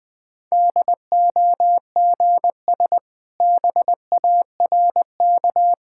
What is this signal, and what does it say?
Morse code, heard as a keyed tone, about 21 wpm, 700 Hz. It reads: DOGS BARK
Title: Easy Sentences with Hints - 20wpm